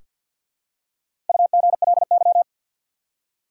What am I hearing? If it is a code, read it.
Morse code, heard as a keyed tone, about 47 wpm, 700 Hz. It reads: UGLY
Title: Four Letter Words - 50wpm